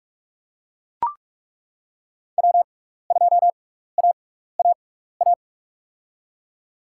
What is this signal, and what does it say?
Morse code, heard as a keyed tone, about 41 wpm, spaced as for 15 wpm, 690 Hz. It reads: W2AAA
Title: USA Call Signs - 5 Characters - 6x Character Spacing - 45wpm